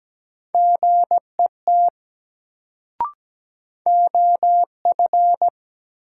Morse code, heard as a keyed tone, about 17 wpm, 700 Hz. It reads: GET OF
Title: Sets of 2 Words - New Word Drill - 17wpm